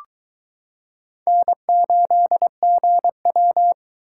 Morse code, heard as a keyed tone, about 22 wpm, 700 Hz. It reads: N8GW